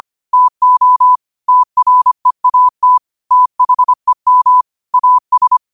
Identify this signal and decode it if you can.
Morse code, heard as a keyed tone, about 25 wpm, 1000 Hz. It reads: TO TREAT THEM AS
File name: qotdfiles / 25wpm